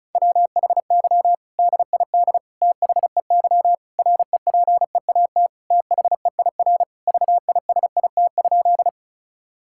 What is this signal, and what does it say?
Morse code, heard as a keyed tone, about 36 wpm, 700 Hz. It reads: WHY DID THEY REPEAT THEIR VISIT?
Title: Sentences from Top 1600 Words 35wpm